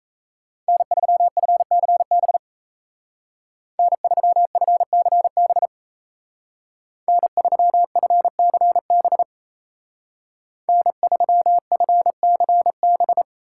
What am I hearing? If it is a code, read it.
Morse code, heard as a keyed tone, about 37 wpm, 700 Hz. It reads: N3FCB N3FCB N3FCB N3FCB